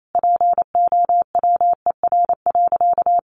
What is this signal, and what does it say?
Morse code, heard as a keyed tone, about 28 wpm, 700 Hz. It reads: POWER.
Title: Sentences from Top 500 Words 28wpm